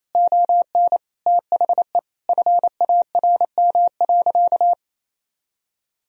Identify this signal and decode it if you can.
Morse code, heard as a keyed tone, about 28 wpm, 700 Hz. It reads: ON THE FARM.
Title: Sentences from Top 1000 Words - Review Part 2 - 28wpm